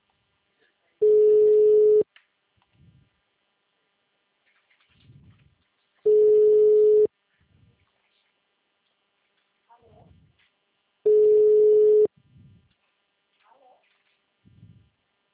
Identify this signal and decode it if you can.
Ringback tone (Europe, 425 Hz), with speech.
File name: out- -2032- - - .7045